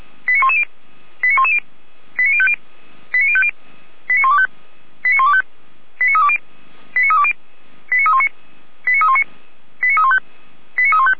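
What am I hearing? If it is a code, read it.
Five-tone selective call (ZVEI), heard as quick runs of five tones, tones of about 70 ms. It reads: 89110 89110 89050 89050 89125 89125 89320 89320 89319 89319 89315 89315
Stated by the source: ZVEI five-tone call 89187 received on SDR.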